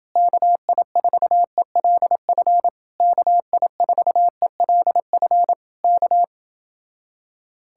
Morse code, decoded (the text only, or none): KI4ELF KI4ELF K